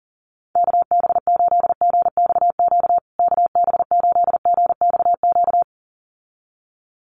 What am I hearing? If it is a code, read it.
Morse code, heard as a keyed tone, about 40 wpm, 700 Hz. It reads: KB8GXQ KB8GXQ
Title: USA Call Signs Repeated 2x 40wpm